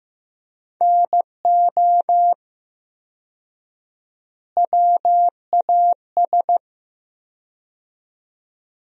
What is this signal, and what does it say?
Morse code, heard as a keyed tone, about 15 wpm, 700 Hz. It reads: NO WAS
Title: 4x Word-Spacing - Sentences from Top 100 Words - 15wpm